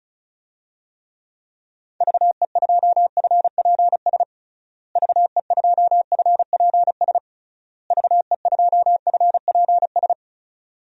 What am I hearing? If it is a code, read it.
Morse code, heard as a keyed tone, about 36 wpm, 700 Hz. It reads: VE2FPS VE2FPS VE2FPS
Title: DXCC-100 Call Signs - 6 Characters - Repeated 3x - 35wpm